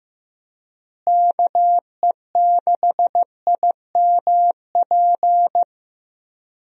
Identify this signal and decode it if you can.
Morse code, heard as a keyed tone, about 15 wpm, 700 Hz. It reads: KE6IMP
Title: USA Call Signs - 6 Characters - Repeated 3x - 15wpm